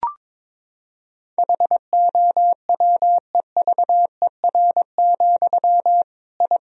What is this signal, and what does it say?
Morse code, heard as a keyed tone, about 22 wpm, 700 Hz. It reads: HOWEVER, I
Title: Sentences from Top 1000 Words - Review Part 1 - 22wpm